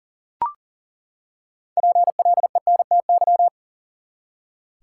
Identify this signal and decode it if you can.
Morse code, heard as a keyed tone, about 40 wpm, 700 Hz. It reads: PLENTY
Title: Six Letter Words - 40wpm